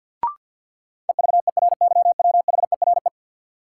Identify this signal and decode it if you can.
Morse code, heard as a keyed tone, about 47 wpm, 700 Hz. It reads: EVERYWHERE
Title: Sentences from Top 1600 Words - New Word Drill Alt - 50wpm